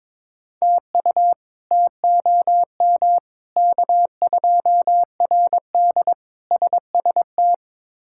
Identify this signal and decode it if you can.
Morse code, heard as a keyed tone, about 21 wpm, 700 Hz. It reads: TU TOM K2RD SST
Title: K1USN SST QSO Practice - 22wpm